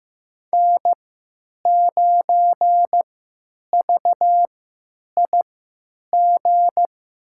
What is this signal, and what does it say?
Morse code, heard as a keyed tone, about 15 wpm, spaced as for 9 wpm, 700 Hz. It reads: N9VIG
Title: USA Call Signs - 3x Character Spacing - 15wpm